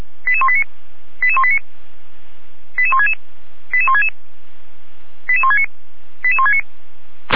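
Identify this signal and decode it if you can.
Five-tone selective call (ZVEI), heard as quick runs of five tones, tones of about 70 ms. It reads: 88189 88189 88166 88166 88169 88169
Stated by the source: Five-tone ZVEI selective call received on an SDR.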